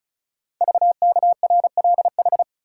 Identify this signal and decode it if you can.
Morse code, heard as a keyed tone, about 36 wpm, 700 Hz. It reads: VKRLH
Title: Code Groups - 5 Letters - 35wpm